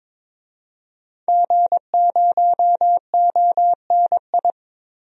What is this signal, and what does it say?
Morse code, heard as a keyed tone, about 22 wpm, 700 Hz. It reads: G0ONI
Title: DXCC-100 Call Signs - 5 Characters - 22wpm